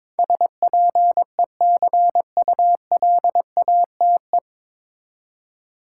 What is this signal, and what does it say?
Morse code, heard as a keyed tone, about 21 wpm, 700 Hz. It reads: SPECULATE
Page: Morse Code Ninja practice 22wpm